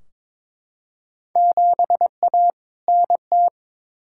Morse code, heard as a keyed tone, about 22 wpm, 700 Hz. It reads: 7A NT
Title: Field Day Exchanges Encore - 22wpm